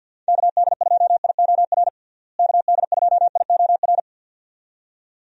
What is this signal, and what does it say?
Morse code, heard as a keyed tone, about 49 wpm, 700 Hz. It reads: KD2IOR KD2IOR
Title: USA Call Signs - 6 Characters - Repeated 2x - 50wpm